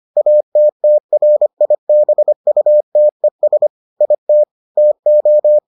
Morse code, heard as a keyed tone, about 25 wpm, 600 Hz. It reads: ATTRIBUTES IT TO